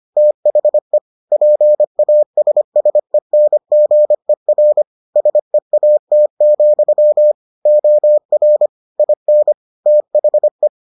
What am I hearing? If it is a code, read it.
Morse code, heard as a keyed tone, about 25 wpm, 600 Hz. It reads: THE PASSENGER SEAT, OR IN THE